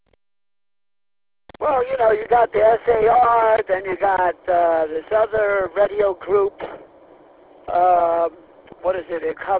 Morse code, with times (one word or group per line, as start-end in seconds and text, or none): none